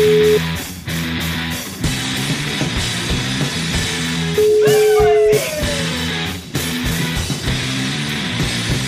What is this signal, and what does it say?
Ringback tone (Europe, 425 Hz), with noise.